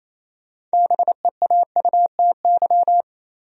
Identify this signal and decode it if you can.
Morse code, heard as a keyed tone, about 35 wpm, 700 Hz. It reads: BEAUTY
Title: Speed-Racing Top 500 Words 28wpm